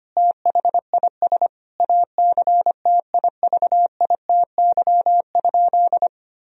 Morse code, heard as a keyed tone, about 25 wpm, 700 Hz. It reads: THIS ACTIVITY?
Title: Sentences from Top 1000 Words - Review Part 4 - 25wpm